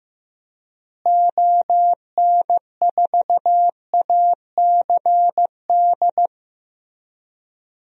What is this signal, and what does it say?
Morse code, heard as a keyed tone, about 15 wpm, 700 Hz. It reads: ON4ACD
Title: DXCC-100 Call Signs - 6 Characters - Repeated 2x - 15wpm